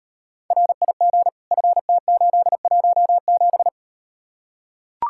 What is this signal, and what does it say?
Morse code, heard as a keyed tone, about 39 wpm, 700 Hz. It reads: RIG FT817